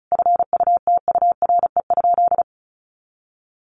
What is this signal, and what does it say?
Morse code, heard as a keyed tone, about 34 wpm, 700 Hz. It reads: FUTURE?
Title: Sentences from Top 700 Words 35wpm